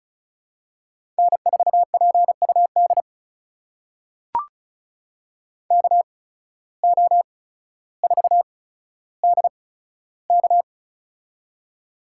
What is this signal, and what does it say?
Morse code, heard as a keyed tone, about 34 wpm, spaced as for 15 wpm, 700 Hz. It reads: N4PUD KO4DK
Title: USA Call Signs - 5 Characters - 8x Character Spacing - 35wpm